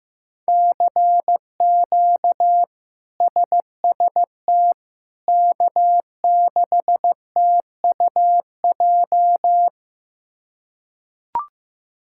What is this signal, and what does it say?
Morse code, heard as a keyed tone, about 15 wpm, 700 Hz. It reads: CQ SST K6TUJ
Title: K1USN SST QSO Practice - 15wpm